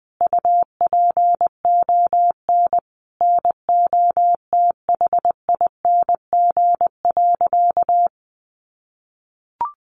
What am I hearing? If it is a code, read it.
Morse code, heard as a keyed tone, about 20 wpm, 700 Hz. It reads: UPON NOTHING.
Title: Sentences from Top 1200 Words 20wpm